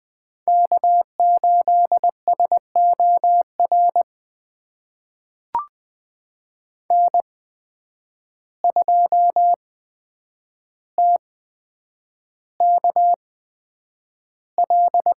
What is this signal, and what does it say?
Morse code, heard as a keyed tone, about 20 wpm, spaced as for 9 wpm, 700 Hz. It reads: K8SOR N2TKL
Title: USA Call Signs - 5 Characters - 8x Character Spacing - 20wpm